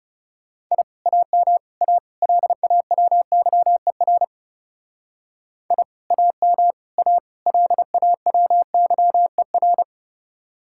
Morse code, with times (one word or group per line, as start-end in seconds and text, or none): 0.7-0.8 I
1.1-1.6 AM
1.8-2.0 A
2.2-4.2 LAWYER
5.7-5.8 I
6.1-6.7 AM
7.0-7.2 A
7.5-9.8 LAWYER